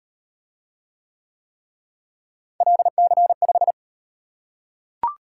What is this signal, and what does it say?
Morse code, heard as a keyed tone, about 38 wpm, 700 Hz. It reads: LC5